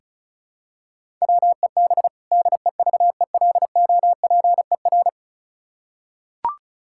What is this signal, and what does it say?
Morse code, heard as a keyed tone, about 34 wpm, 700 Hz. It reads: WEB DEVELOPER
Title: K1USN SST QSO Practice 35wpm